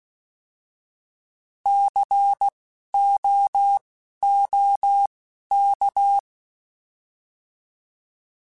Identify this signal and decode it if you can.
Morse code, heard as a keyed tone, about 16 wpm, spaced as for 11 wpm, 800 Hz. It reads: COOK
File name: Alphabetical / Alphabetical mp3s / Cs @10